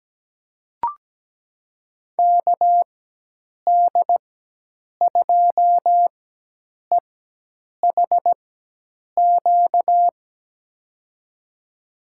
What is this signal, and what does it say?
Morse code, heard as a keyed tone, about 17 wpm, spaced as for 8 wpm, 700 Hz. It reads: KD2EHQ
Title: USA Call Signs - 6 Characters - 4x Character Spacing - 17wpm